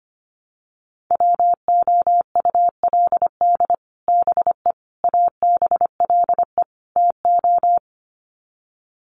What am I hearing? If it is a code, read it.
Morse code, heard as a keyed tone, about 25 wpm, 700 Hz. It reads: WOULD BE ABLE TO
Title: Sets of 4 Words 25wpm